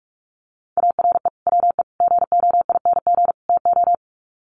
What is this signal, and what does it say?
Morse code, heard as a keyed tone, about 44 wpm, 700 Hz. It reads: ARE WE GOING TO
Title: Sets of 4 Words 45wpm